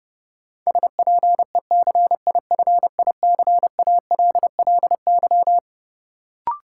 Morse code, heard as a keyed tone, about 30 wpm, 700 Hz. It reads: SPECIFICALLY
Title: Sentences from Top 1100 Words - New Word Drill - 30wpm